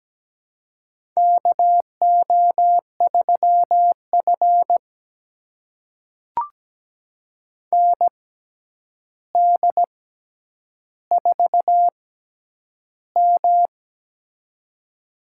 Morse code, heard as a keyed tone, about 17 wpm, spaced as for 9 wpm, 700 Hz. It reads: KO3F ND4M
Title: USA Call Signs - 4 Characters - 6x Character Spacing - 17wpm